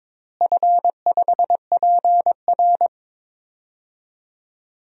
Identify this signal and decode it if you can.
Morse code, heard as a keyed tone, about 22 wpm, 700 Hz. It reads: F5PR
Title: DXCC-100 Call Signs - 4 Characters - Repeated 3x - 22wpm